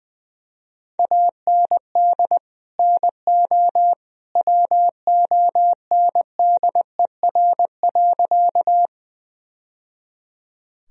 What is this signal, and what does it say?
Morse code, heard as a keyed tone, about 20 wpm, 700 Hz. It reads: AND NO WONDER.
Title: Sentences from Top 1000 Words - Review Part 3 - 20wpm